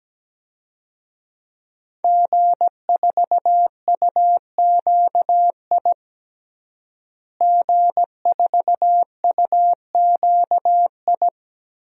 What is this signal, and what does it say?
Morse code, heard as a keyed tone, about 17 wpm, 700 Hz. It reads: G4UQI G4UQI
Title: DXCC-100 Call Signs - 5 Characters - Repeated 2x - 17wpm